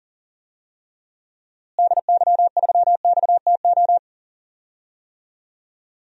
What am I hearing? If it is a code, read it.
Morse code, heard as a keyed tone, about 40 wpm, 700 Hz. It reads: DY3XTO
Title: DXCC-100 Call Signs - 6 Characters - Repeated 3x - 40wpm